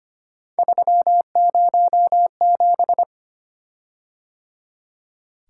Morse code, heard as a keyed tone, about 25 wpm, 700 Hz. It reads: 307
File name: Three Letter-Number Combinations - 25wpm